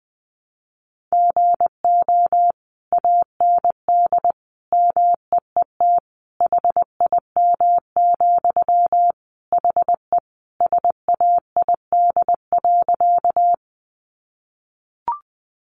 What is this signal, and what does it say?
Morse code, heard as a keyed tone, about 19 wpm, 700 Hz. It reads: GO AND MEET HIM, HE SAID.
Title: Sentences from Top 1100 Words 20wpm